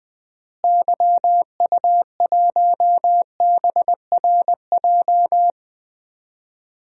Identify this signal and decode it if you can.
Morse code, heard as a keyed tone, about 20 wpm, 700 Hz. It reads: YU1BRJ
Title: DXCC-100 Call Signs - 6 Characters - Repeated 2x - 20wpm